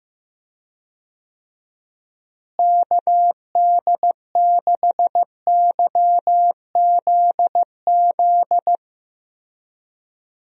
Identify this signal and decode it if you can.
Morse code, heard as a keyed tone, about 15 wpm, 700 Hz. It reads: KD6YZZ
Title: USA Call Signs - 6 Characters - Repeated 3x - 15wpm